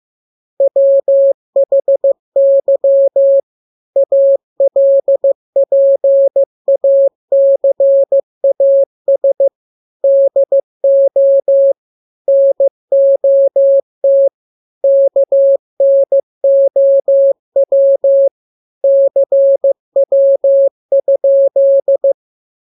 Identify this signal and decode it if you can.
Morse code, heard as a keyed tone, about 15 wpm, 550 Hz. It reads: WHY ALPACAS DO NOT KNOW CW?
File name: jokes / joke 14 15 0000